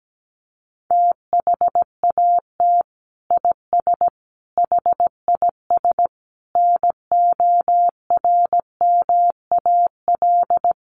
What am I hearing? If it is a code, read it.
Morse code, heard as a keyed tone, about 17 wpm, 700 Hz. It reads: THAT IS HIS NORMAL
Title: Sentences from Top 1200 Words 17wpm